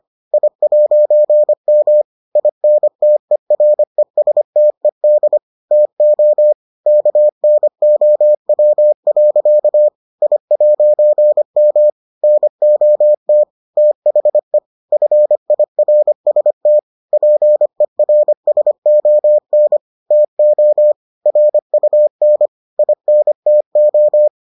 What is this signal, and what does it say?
Morse code, heard as a keyed tone, about 25 wpm, 600 Hz. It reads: I'M INTERESTED TO KNOW. I'M NOT THE FIRST PERSON TO RUN INTO